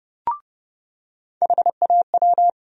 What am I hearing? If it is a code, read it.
Morse code, heard as a keyed tone, about 30 wpm, 700 Hz. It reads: HAW
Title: USA Call Sign Postfixes - 3 Letters 30wpm